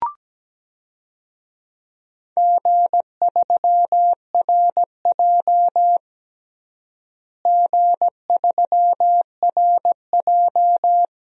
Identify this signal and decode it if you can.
Morse code, heard as a keyed tone, about 17 wpm, 700 Hz. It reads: G3RJ G3RJ
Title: DXCC-100 Call Signs - 4 Characters - Repeated 3x - 17wpm